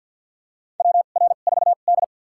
Morse code, heard as a keyed tone, about 52 wpm, spaced as for 36 wpm, 700 Hz. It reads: WR4D
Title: USA Call Signs - 4 Characters - 2x Character Spacing - 50wpm